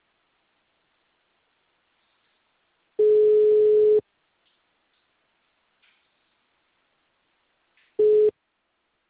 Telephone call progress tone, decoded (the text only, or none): ringback tone